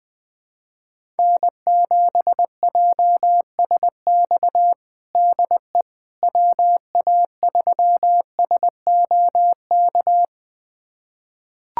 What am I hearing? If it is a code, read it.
Morse code, heard as a keyed tone, about 20 wpm, 700 Hz. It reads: N7JSX DE WA3SOK
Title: USA Callsign DE Callsign 20wpm